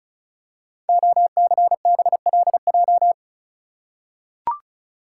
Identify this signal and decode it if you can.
Morse code, heard as a keyed tone, about 36 wpm, 690 Hz. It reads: OCBLJ
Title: Code Groups - 5 Letters - 35wpm